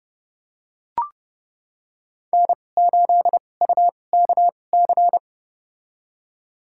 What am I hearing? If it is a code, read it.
Morse code, heard as a keyed tone, about 32 wpm, spaced as for 22 wpm, 700 Hz. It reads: N8UKC
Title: USA Call Signs - 5 Characters - 2x Character Spacing - 30wpm